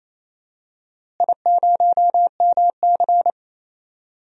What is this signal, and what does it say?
Morse code, heard as a keyed tone, about 28 wpm, 700 Hz. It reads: I0MC
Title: DXCC-100 Call Signs - 4 Characters - Repeated 3x - 28wpm